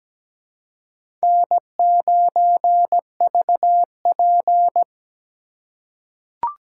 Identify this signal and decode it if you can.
Morse code, heard as a keyed tone, about 17 wpm, 700 Hz. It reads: N9VP
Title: USA Call Signs - 4 Characters - 4x Character Spacing - 17wpm